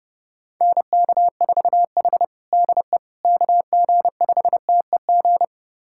Morse code, heard as a keyed tone, about 30 wpm, 700 Hz. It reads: NK4H DE KG5TEG
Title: USA Callsign DE Callsign 30wpm